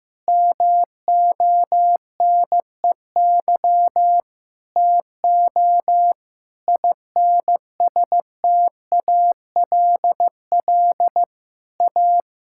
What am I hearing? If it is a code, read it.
Morse code, heard as a keyed tone, about 16 wpm, 700 Hz. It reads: MONEY TO INSTALL A
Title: Sentences from Top 1600 Words 15wpm